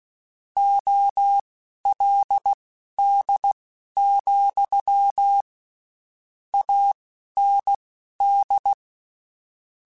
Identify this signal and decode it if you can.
Morse code, heard as a keyed tone, about 16 wpm, spaced as for 11 wpm, 800 Hz. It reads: OLD, AND
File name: (spaced) / QSOs Spaced-mp3s / kc7haa @15 sp